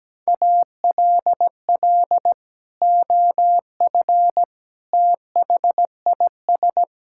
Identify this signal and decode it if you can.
Morse code, heard as a keyed tone, about 17 wpm, 700 Hz. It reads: ALL OF THIS
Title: Sentences from Top 1000 Words - Review Part 6 - 17wpm